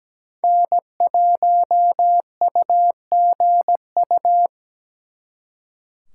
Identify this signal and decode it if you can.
Morse code, heard as a keyed tone, about 20 wpm, 700 Hz. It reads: N1UGU
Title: Speed Racing - USA 1x3 Call Signs 17wpm